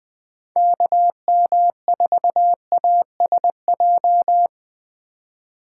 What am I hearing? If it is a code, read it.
Morse code, heard as a keyed tone, about 20 wpm, 700 Hz. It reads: KM4ASJ
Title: Speed Racing - USA 2x3 Call Signs 17wpm